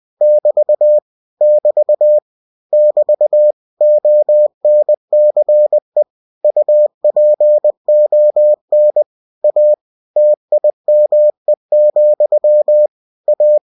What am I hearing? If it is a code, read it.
Morse code, heard as a keyed tone, about 20 wpm, 600 Hz. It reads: <BT> <BT> <BT> ONCE UPON A TIME, A